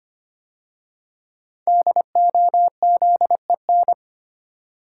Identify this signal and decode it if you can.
Morse code, heard as a keyed tone, about 25 wpm, 700 Hz. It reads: DOZEN
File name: 5 Letter Words - First 3 Letters Given - 25wpm